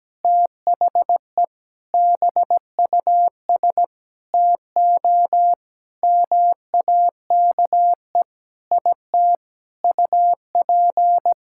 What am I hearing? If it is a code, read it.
Morse code, heard as a keyed tone, about 17 wpm, 700 Hz. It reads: THE BUS TO MAKE IT UP
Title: Sentences from Top 1000 Words - Review Part 6 - 17wpm